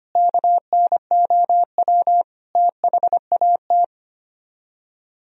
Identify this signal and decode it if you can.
Morse code, heard as a keyed tone, about 32 wpm, 700 Hz. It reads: KNOW THAT